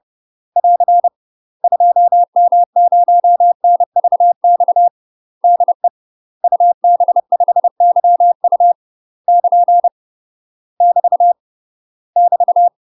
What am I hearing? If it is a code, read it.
Morse code, heard as a keyed tone, about 30 wpm, 700 Hz. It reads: <AR> 2M0NVX DE UB5YU <KN> <BT> <BT>
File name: UK MASTER QSOs TRAINING / 30WPM / UK MASTER QSOs 0010